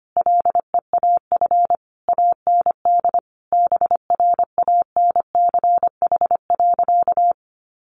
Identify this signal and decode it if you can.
Morse code, heard as a keyed tone, about 25 wpm, 700 Hz. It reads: LEAF AND BRANCH.
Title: A Sound Of Thunder 25wpm